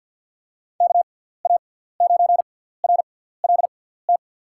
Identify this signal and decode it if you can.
Morse code, heard as a keyed tone, about 53 wpm, spaced as for 17 wpm, 700 Hz. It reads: KA9RLT
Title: USA Call Signs - 6 Characters - 6x Character Spacing - 50wpm